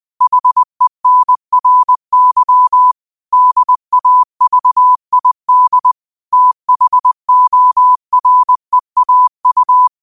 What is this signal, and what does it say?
Morse code, heard as a keyed tone, about 20 wpm, 1000 Hz. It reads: HENRY DAVID THOREAU
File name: qotdfiles / 20wpm